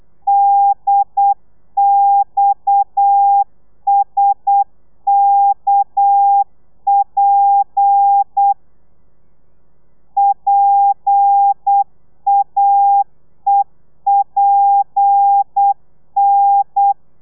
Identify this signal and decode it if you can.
Morse code, heard as a keyed tone, about 8 wpm, 800 Hz. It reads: DXSKP PAEPN